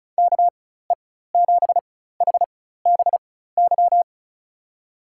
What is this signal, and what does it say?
Morse code, heard as a keyed tone, about 36 wpm, spaced as for 16 wpm, 700 Hz. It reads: KE7HBY
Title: USA Call Signs - 6 Characters - 4x Character Spacing - 35wpm